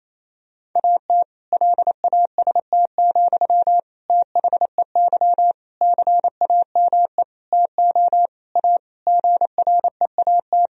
Morse code, heard as a keyed tone, about 28 wpm, 700 Hz. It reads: AT LAST, THEY CAME TO A GREAT
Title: Sentences from Top 1000 Words - Review Part 1 - 28wpm